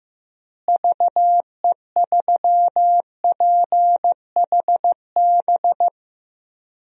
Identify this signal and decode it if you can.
Morse code, heard as a keyed tone, about 16 wpm, 700 Hz. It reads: VE3PHB
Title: DXCC-100 Call Signs - 6 Characters - Repeated 3x - 15wpm